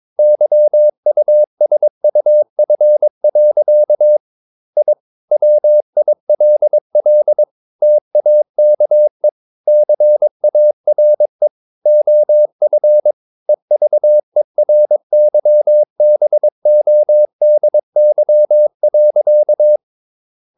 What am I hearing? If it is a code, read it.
Morse code, heard as a keyed tone, about 22 wpm, 600 Hz. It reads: YUSUF. I WILL TAKE CARE OF EVERYBODY.
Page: Romanen Never af Ken Follet på engelsk.